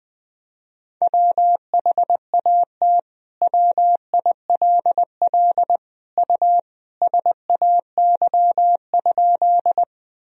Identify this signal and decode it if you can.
Morse code, heard as a keyed tone, about 21 wpm, 700 Hz. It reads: WHAT WILL U SAY?
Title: Sentences from Top 200 Words and Abbreviations 20wpm